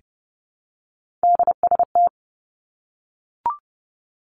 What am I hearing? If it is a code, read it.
Morse code, heard as a keyed tone, about 30 wpm, 700 Hz. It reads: DST
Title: Three Challenging Letter-Number Combinations 30wpm